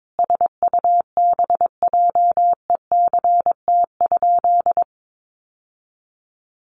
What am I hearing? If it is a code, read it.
Morse code, heard as a keyed tone, about 22 wpm, 700 Hz. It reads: SUBJECT?
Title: Sentences from Top 1200 Words 22wpm